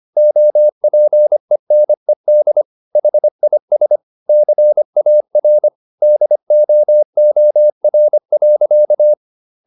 Morse code, heard as a keyed tone, about 25 wpm, 600 Hz. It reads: OPENED HIS CAR DOOR.